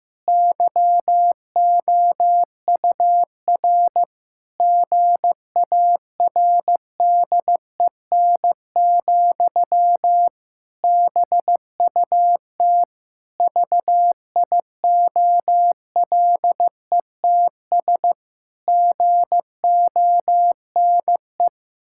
Morse code, heard as a keyed tone, about 15 wpm, 700 Hz. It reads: YOUR GARDEN, BUT VIOLETS GONE
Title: Pangram Challenge v2 - 15wpm